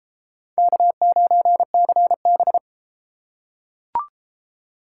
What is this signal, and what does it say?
Morse code, heard as a keyed tone, about 33 wpm, 700 Hz. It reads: K9CB